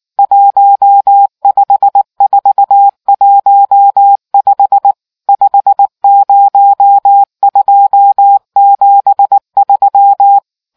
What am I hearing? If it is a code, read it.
Morse code, heard as a keyed tone, about 19 wpm, 800 Hz. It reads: 15415 50273